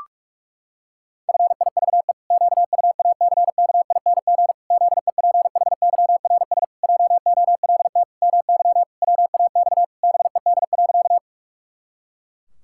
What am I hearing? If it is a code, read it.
Morse code, heard as a keyed tone, about 46 wpm, 700 Hz. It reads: FIVE QUACKING ZEPHYRS JOLT MY WAX BED.
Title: Pangram Challenge v2 - 45wpm